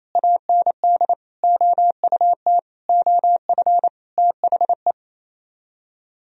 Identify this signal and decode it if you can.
Morse code, heard as a keyed tone, about 35 wpm, 700 Hz. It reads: AND OUT OF THE
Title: Speed-Racing - Sets of 5 Words 28wpm